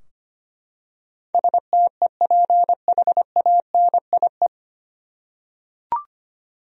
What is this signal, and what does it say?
Morse code, heard as a keyed tone, about 25 wpm, 700 Hz. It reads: STEPHANIE
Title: USA Common Names - 25wpm